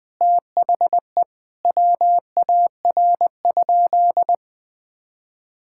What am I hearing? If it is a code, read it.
Morse code, heard as a keyed tone, about 21 wpm, 700 Hz. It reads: THE WAR?
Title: Sentences from Top 1000 Words - Review Part 5 - 20wpm